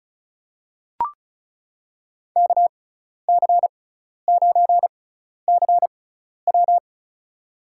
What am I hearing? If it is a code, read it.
Morse code, heard as a keyed tone, about 36 wpm, spaced as for 12 wpm, 700 Hz. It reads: KC9CW
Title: USA Call Signs - 5 Characters - 6x Character Spacing - 35wpm